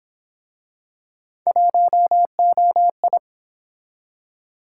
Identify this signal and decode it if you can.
Morse code, heard as a keyed tone, about 32 wpm, 700 Hz. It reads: 1OI